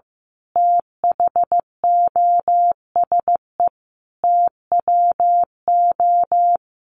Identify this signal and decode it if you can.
Morse code, heard as a keyed tone, about 14 wpm, 700 Hz. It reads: THOSE TWO
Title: A Sound Of Thunder 15wpm